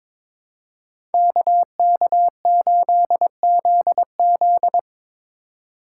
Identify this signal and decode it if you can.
Morse code, heard as a keyed tone, about 26 wpm, 700 Hz. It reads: KK8ZZ
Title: Speed Racing - USA 2x2 Call Signs 22wpm